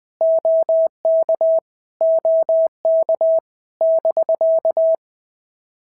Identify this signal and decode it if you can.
Morse code, heard as a keyed tone, about 20 wpm, 660 Hz. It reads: OK OK <BK>
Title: POTA QSO Practice - 20wpm